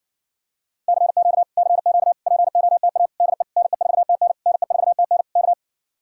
Morse code, heard as a keyed tone, about 55 wpm, 700 Hz. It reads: CQ CQ POTA DE NE4TN NE4TN K
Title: Speed-Racing Common QSO Phrases Encore - 50wpm